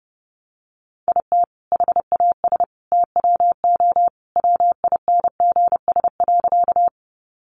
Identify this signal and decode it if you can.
Morse code, heard as a keyed tone, about 28 wpm, 700 Hz. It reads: IT HAS TWO WINGS.
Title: Additional Easy Phrases 30wpm